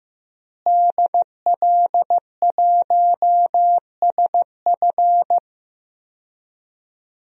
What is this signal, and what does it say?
Morse code, heard as a keyed tone, about 16 wpm, 700 Hz. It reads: DL1SF
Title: DXCC-100 Call Signs - 5 Characters - Repeated 2x - 15wpm